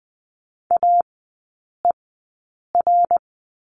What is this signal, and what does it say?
Morse code, heard as a keyed tone, about 20 wpm, spaced as for 8 wpm, 700 Hz. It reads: AER
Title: 053 Lesson 10e - Letter r - Three Characters 20wpm